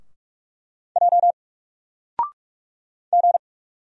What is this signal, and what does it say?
Morse code, heard as a keyed tone, about 44 wpm, 700 Hz. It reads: J G
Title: Single Character (Letter:Number:Punctuation:Prosign) 45wpm